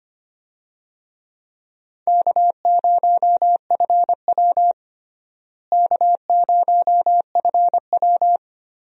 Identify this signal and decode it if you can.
Morse code, heard as a keyed tone, about 25 wpm, 700 Hz. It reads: K0FW K0FW
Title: USA Call Signs - 4 Characters - Repeated 2x - 25wpm